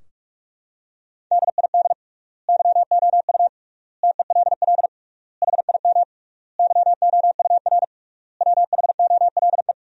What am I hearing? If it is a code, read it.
Morse code, heard as a keyed tone, about 46 wpm, 700 Hz. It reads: DID YOU TELL HIM YOUR WHOLE
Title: 3x Word-Spacing - Sentences from Top 400 Words - 45wpm